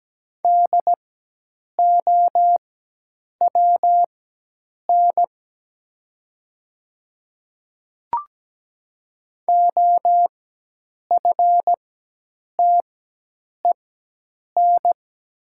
Morse code, heard as a keyed tone, about 17 wpm, spaced as for 8 wpm, 700 Hz. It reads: DOWN OFTEN
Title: 4x Character Spacing - Top 500 Words - 17wpm